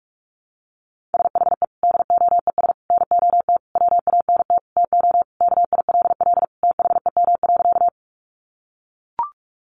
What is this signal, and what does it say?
Morse code, heard as a keyed tone, about 44 wpm, 700 Hz. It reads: SHE DOES NOT WANT TO KILL THEM.
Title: Easy Phrases 45wpm